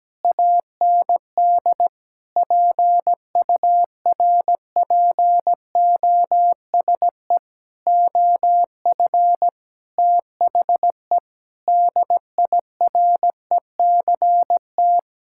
Morse code, heard as a keyed tone, about 17 wpm, 700 Hz. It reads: AND PURPOSE OF THE DIRECT
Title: Sentences from Top 1000 Words - Review Part 5 - 17wpm